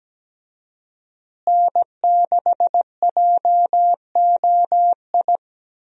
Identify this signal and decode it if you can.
Morse code, heard as a keyed tone, about 17 wpm, 700 Hz. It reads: N6JOI
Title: USA Call Signs - 5 Characters - Repeated 2x - 17wpm